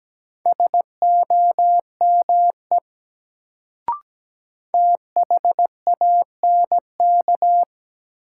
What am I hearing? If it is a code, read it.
Morse code, heard as a keyed tone, about 17 wpm, 700 Hz. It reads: SOME THANK
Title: Sets of 2 Words - New Word Drill - 17wpm